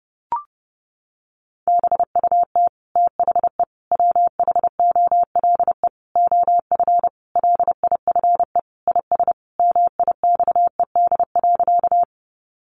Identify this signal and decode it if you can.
Morse code, heard as a keyed tone, about 30 wpm, 700 Hz. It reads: BUT THE WHOLE OF LIFE IS MIXED.
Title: Sentences from Top 1500 Words 30wpm